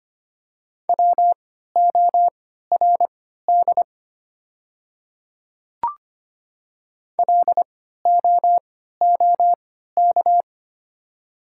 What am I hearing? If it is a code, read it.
Morse code, heard as a keyed tone, about 25 wpm, spaced as for 14 wpm, 700 Hz. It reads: WORD LOOK
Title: 3x Character Spacing - Top 100 Words - 25wpm